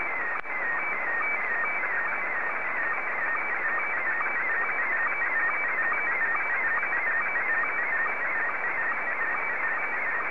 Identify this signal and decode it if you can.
SSTV (slow-scan television), Scottie 1.